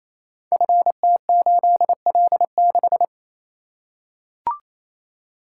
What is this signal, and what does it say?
Morse code, heard as a keyed tone, about 27 wpm, 700 Hz. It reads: FT8L6
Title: Code Groups - 5 Letter:Numbers - 28wpm